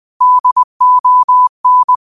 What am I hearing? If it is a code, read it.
Morse code, heard as a keyed tone, about 20 wpm, 1000 Hz. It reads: DON
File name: qotdfiles / 20wpm